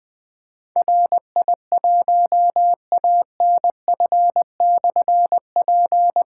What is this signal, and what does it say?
Morse code, heard as a keyed tone, about 20 wpm, 700 Hz. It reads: RI1ANF/P
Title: Difficult Call Signs 20wpm